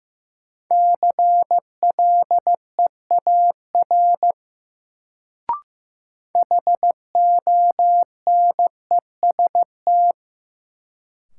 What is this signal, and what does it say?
Morse code, heard as a keyed tone, about 15 wpm, 700 Hz. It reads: CLEAR HONEST
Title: Sets of 5 Words Encore - New Word Drill - 15wpm